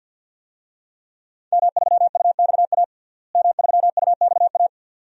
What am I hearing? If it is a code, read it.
Morse code, heard as a keyed tone, about 50 wpm, 700 Hz. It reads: M3UXA M3UXA
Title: DXCC-100 Call Signs - 5 Characters - Repeated 2x - 50wpm